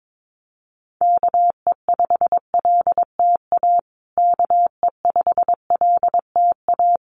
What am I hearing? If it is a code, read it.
Morse code, heard as a keyed tone, about 22 wpm, 700 Hz. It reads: KE5LTA KE5LTA
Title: USA Call Signs Repeated 3x 22wpm